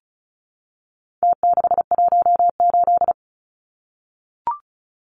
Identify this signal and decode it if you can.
Morse code, heard as a keyed tone, about 34 wpm, 700 Hz. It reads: T618
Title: Numbers - Contest Style - 4-Digit 35wpm